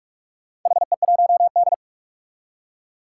Morse code, heard as a keyed tone, about 41 wpm, 700 Hz. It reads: HE1D
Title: Four Challenging Letter-Number Combinations 45wpm